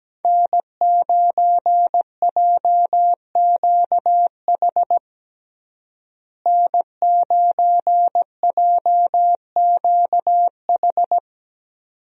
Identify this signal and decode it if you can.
Morse code, heard as a keyed tone, about 17 wpm, 700 Hz. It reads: N9JQH N9JQH
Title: USA Call Signs - 5 Characters - Repeated 2x - 17wpm